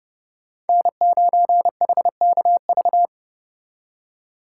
Morse code, heard as a keyed tone, about 36 wpm, 700 Hz. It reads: N9HKV
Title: Speed Racing - USA 1x3 Call Signs 30wpm